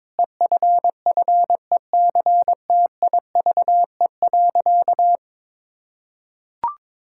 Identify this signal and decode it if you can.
Morse code, heard as a keyed tone, about 22 wpm, 700 Hz. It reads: EFFECTIVE.
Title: Sentences from Top 1000 Words - Review Part 6 - 22wpm